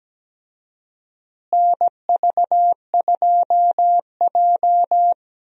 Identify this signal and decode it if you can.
Morse code, heard as a keyed tone, about 17 wpm, 700 Hz. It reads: NV2J
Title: USA Call Signs - 4 Characters - Repeated 3x - 17wpm